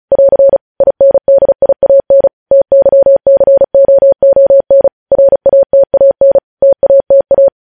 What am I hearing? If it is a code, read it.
Morse code, heard as a keyed tone, about 36 wpm, 550 Hz. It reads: <AR> INDIAN TYCOON RATAN TATA